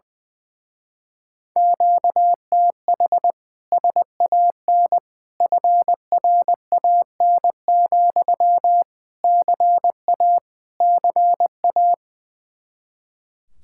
Morse code, heard as a keyed tone, about 20 wpm, 700 Hz. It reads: QTH SAN FRAN, CA CA
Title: Speed-Racing Common QSO Phrases - 20wpm